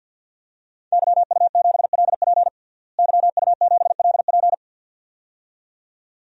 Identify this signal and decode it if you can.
Morse code, heard as a keyed tone, about 50 wpm, 700 Hz. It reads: YU7LP YU7LP
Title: DXCC-100 Call Signs - 5 Characters - Repeated 2x - 50wpm